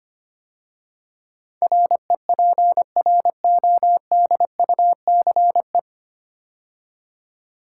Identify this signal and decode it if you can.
Morse code, heard as a keyed tone, about 25 wpm, 700 Hz. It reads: REPRODUCE
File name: practice / 25wpm / 9 Letter Words - First 3 Letters Given - 25wpm